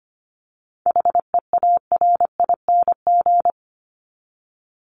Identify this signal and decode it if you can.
Morse code, heard as a keyed tone, about 25 wpm, 700 Hz. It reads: HEARING
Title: Top 1751-2000 Words 25wpm